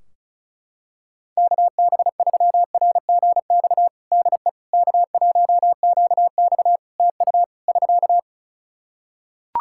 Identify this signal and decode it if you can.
Morse code, heard as a keyed tone, about 34 wpm, 700 Hz. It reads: KB3RGX DE K1QX TU <SK>
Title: Practice QSO - Exchange 4 - Ending - 35wpm